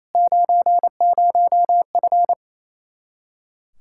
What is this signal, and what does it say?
Morse code, heard as a keyed tone, about 28 wpm, 700 Hz. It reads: 90F
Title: Practice QSO - Exchange 2 - Info - 28wpm